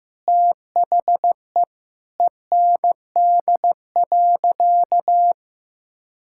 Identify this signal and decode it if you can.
Morse code, heard as a keyed tone, about 15 wpm, 700 Hz. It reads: THE END.
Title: Sentences from Top 1000 Words - Review Part 5 - 15wpm